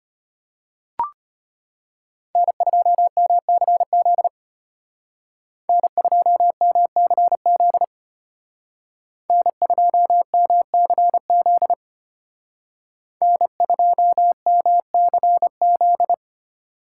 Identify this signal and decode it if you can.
Morse code, heard as a keyed tone, about 31 wpm, 700 Hz. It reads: N2MCZ N2MCZ N2MCZ N2MCZ
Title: Speed Racing - USA 1x3 Call Signs 25wpm